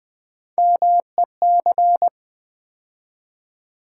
Morse code, heard as a keyed tone, about 20 wpm, 700 Hz. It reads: MEC
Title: Build A Word - 3 Letters At A Time - 9-Letter Words - 20wpm